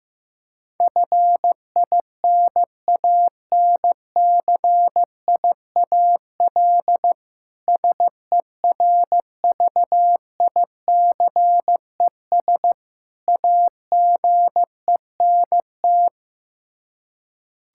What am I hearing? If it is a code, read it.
Morse code, heard as a keyed tone, about 15 wpm, 700 Hz. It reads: FINANCIAL SERVICES AGENT
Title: Practice QSO - Exchange 3 - More Info - 15wpm